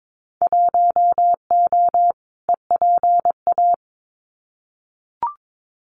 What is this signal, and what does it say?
Morse code, heard as a keyed tone, about 22 wpm, 700 Hz. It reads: 1O EPA
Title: Winter Field Day Exchanges 22wpm